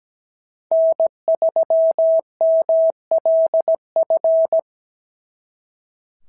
Morse code, heard as a keyed tone, about 17 wpm, 690 Hz. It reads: N3MLF
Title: SOTA QSO Practice 17wpm